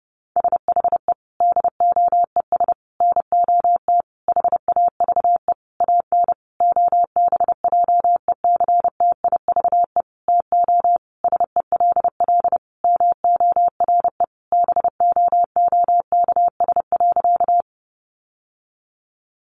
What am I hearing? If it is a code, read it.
Morse code, heard as a keyed tone, about 30 wpm, 700 Hz. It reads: SHE DOES NOT HAVE AN OBJECTIVE TO SELL MORE BOOKS.
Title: Sentences from Top 1200 Words 30wpm